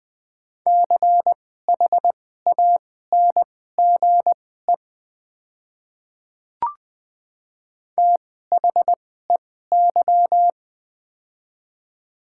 Morse code, heard as a keyed tone, about 21 wpm, spaced as for 15 wpm, 700 Hz. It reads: CHANGE THEY
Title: 2x Character Spacing - Top 200 Words - 20wpm